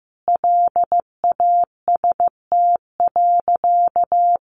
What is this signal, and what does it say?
Morse code, heard as a keyed tone, about 15 wpm, 700 Hz. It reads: LAST.
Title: Sentences from Top 300 Words 15wpm